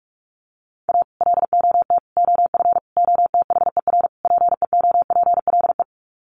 Morse code, heard as a keyed tone, about 44 wpm, 700 Hz. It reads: A LOT OF OTHER PEOPLE
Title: Sets of 5 Words Encore2 45wpm